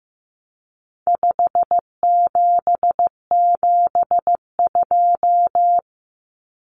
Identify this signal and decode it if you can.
Morse code, heard as a keyed tone, about 15 wpm, 700 Hz. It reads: 5772
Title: Common Cut Numbers - 4-Digit 15wpm